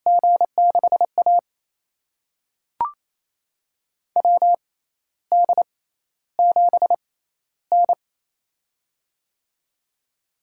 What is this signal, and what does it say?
Morse code, heard as a keyed tone, about 28 wpm, spaced as for 13 wpm, 700 Hz. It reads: G6A WD7N
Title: USA Call Signs - 6x Character Spacing - 28wpm